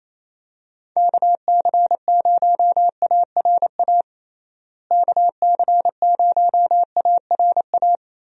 Morse code, heard as a keyed tone, about 28 wpm, 700 Hz. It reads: KC0ARA KC0ARA
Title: USA Call Signs - 6 Characters - Repeated 2x - 28wpm